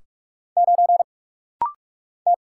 Morse code, heard as a keyed tone, about 46 wpm, 700 Hz. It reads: 9 T
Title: Single Letter-Number - Rapid-Fire 45wpm